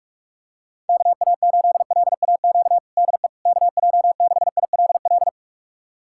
Morse code, heard as a keyed tone, about 47 wpm, 700 Hz. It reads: KA8LAQ DE KJ6ILL
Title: USA Callsign DE Callsign 45wpm